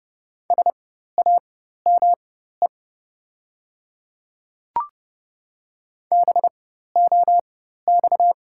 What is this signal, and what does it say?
Morse code, heard as a keyed tone, about 30 wpm, spaced as for 14 wpm, 700 Hz. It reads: SAME BOX
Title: 4x Character Spacing - Top 500 Words - 30wpm